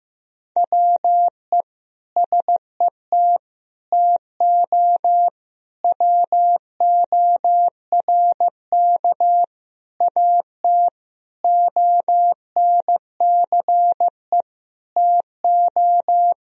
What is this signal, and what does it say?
Morse code, heard as a keyed tone, about 15 wpm, 700 Hz. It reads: WE SET TO WORK AT ONCE TO
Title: Sentences from Top 1000 Words - Review Part 1 - 15wpm